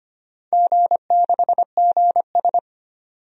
Morse code, heard as a keyed tone, about 25 wpm, 700 Hz. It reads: G6GS
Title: DXCC-100 Call Signs - 4 Characters - 25wpm